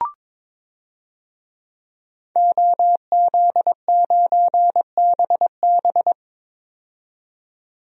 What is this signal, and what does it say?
Morse code, heard as a keyed tone, about 21 wpm, 700 Hz. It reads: OZ9BB
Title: DXCC-100 Call Signs - 5 Characters - 22wpm